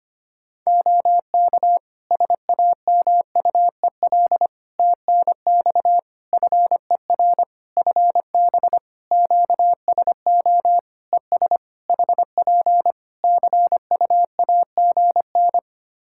Morse code, heard as a keyed tone, about 25 wpm, 700 Hz. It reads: OK SAMUEL TNX FER FB QSO ES HP CUAGN
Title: K1USN SST QSO Practice 25wpm